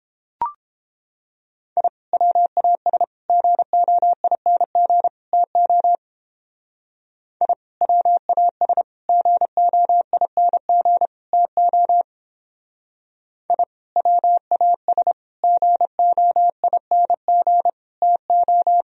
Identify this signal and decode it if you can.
Morse code, heard as a keyed tone, about 29 wpm, 700 Hz. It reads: I WAS GOING TO I WAS GOING TO I WAS GOING TO